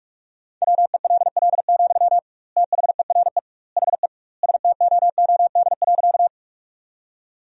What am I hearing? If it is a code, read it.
Morse code, heard as a keyed tone, about 46 wpm, 700 Hz. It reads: WELL, THERE HE STOOD.
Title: 2x Word-Spacing - Sentences from Top 500 Words - 45wpm